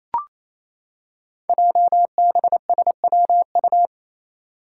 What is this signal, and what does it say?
Morse code, heard as a keyed tone, about 28 wpm, 700 Hz. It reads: JBSWU
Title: Code Groups - 5 Letter:Numbers - 28wpm